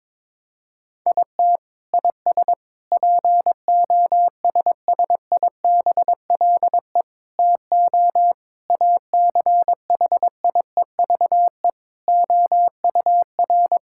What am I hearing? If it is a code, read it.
Morse code, heard as a keyed tone, about 22 wpm, 700 Hz. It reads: IT IS POSSIBLE TO ACHIEVE OUR
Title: Sentences from Top 1000 Words - Review Part 6 - 22wpm